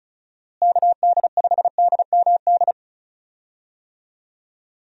Morse code, heard as a keyed tone, about 34 wpm, 700 Hz. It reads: KD5DMD
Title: USA Call Signs - 6 Characters - Repeated 2x - 35wpm